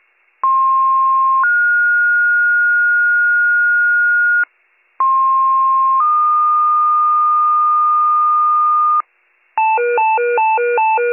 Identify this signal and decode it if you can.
Two-tone page, heard as a one-second tone, then a three-second tone, about 1060 then 1430 Hz, about 1060 then 1220 Hz, with noise. It is heard from 0.4 to 9.0 s.